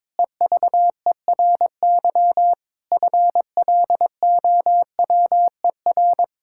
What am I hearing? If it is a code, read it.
Morse code, heard as a keyed tone, about 22 wpm, 700 Hz. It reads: EVERY FLOWER
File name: Difficult Sentences with Hints - 22wpm